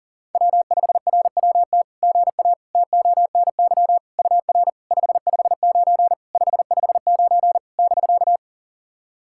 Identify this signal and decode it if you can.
Morse code, heard as a keyed tone, about 40 wpm, 700 Hz. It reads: W5RWT GA TONY UR 559 559 <BK>
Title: SOTA QSO Practice 40wpm